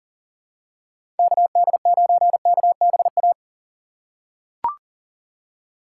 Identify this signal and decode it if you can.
Morse code, heard as a keyed tone, about 40 wpm, 700 Hz. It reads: KD9KBA
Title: USA Call Signs - 3x Character Spacing - 40wpm